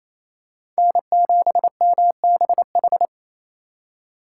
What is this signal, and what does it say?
Morse code, heard as a keyed tone, about 28 wpm, 700 Hz. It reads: N7MBH
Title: USA Call Signs - 3x Character Spacing - 28wpm